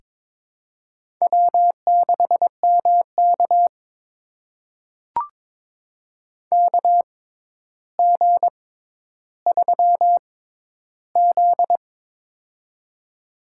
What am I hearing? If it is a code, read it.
Morse code, heard as a keyed tone, about 22 wpm, spaced as for 11 wpm, 700 Hz. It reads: W6MK KG3Z
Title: USA Call Signs - 4 Characters - 6x Character Spacing - 22wpm